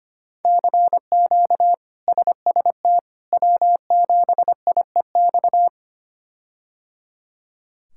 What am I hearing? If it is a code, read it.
Morse code, heard as a keyed tone, about 25 wpm, 700 Hz. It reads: CQ SST W7IEX
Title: K1USN SST QSO Practice - 25wpm